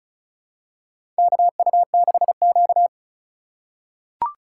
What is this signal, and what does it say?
Morse code, heard as a keyed tone, about 36 wpm, 700 Hz. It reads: KU6Q
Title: USA Call Signs - 4 Characters - 2x Character Spacing - 35wpm